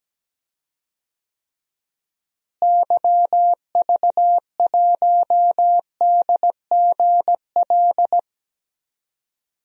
Morse code, heard as a keyed tone, about 17 wpm, 700 Hz. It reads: YV1DGL
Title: DXCC-100 Call Signs - 6 Characters - Repeated 3x - 17wpm